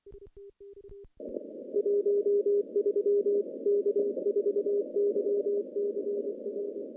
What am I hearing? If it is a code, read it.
Morse code, heard as a keyed tone, about 24 wpm, 410 Hz. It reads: 13 K4YCR
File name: contests / CW-SS-k5tr- -0100